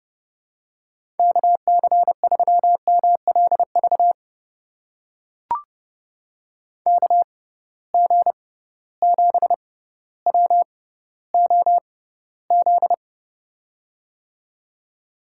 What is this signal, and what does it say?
Morse code, heard as a keyed tone, about 32 wpm, spaced as for 16 wpm, 700 Hz. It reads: KC3MLV KG7WOZ
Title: USA Call Signs - 6 Characters - 6x Character Spacing - 30wpm